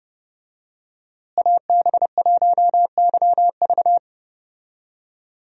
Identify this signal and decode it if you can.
Morse code, heard as a keyed tone, about 30 wpm, 700 Hz. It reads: AB1YV
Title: USA Call Signs - 5 Characters - Repeated 2x - 30wpm